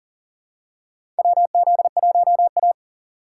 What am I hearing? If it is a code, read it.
Morse code, heard as a keyed tone, about 38 wpm, 700 Hz. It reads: WZ1A
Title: USA Call Signs - 4 Characters - 6x Character Spacing - 40wpm